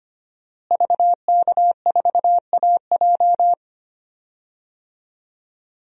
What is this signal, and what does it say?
Morse code, heard as a keyed tone, about 25 wpm, 700 Hz. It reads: VK4AJ
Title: DXCC-100 Call Signs - 5 Characters - Repeated 2x - 25wpm